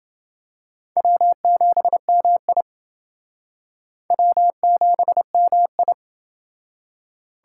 Morse code, heard as a keyed tone, about 30 wpm, 700 Hz. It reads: W7MI W7MI